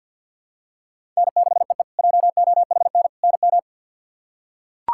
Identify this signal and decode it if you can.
Morse code, heard as a keyed tone, about 53 wpm, 700 Hz. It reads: N6EE JOHN NM
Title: K1USN SST QSO Practice - 50wpm